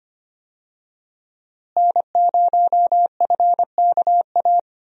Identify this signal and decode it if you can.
Morse code, heard as a keyed tone, about 25 wpm, 700 Hz. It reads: N0FKA
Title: USA Call Signs - 5 Characters - Repeated 3x - 25wpm